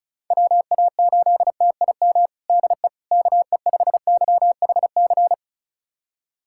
Morse code, heard as a keyed tone, about 36 wpm, 700 Hz. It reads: WA8TIM DE KE5YHC
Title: USA Callsign DE Callsign 35wpm